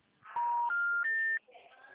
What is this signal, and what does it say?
Special information tone, with speech.
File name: out- -2048- - - .21991